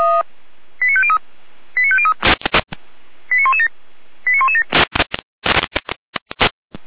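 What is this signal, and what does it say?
Five-tone selective call (ZVEI), heard as quick runs of five tones, tones of about 70 ms. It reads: 89503 89503 89117 89117
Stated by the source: ZVEI five-tone selective call 89164, received on SDR by feuerwehr-orlamuende (fire brigade Orlamünde).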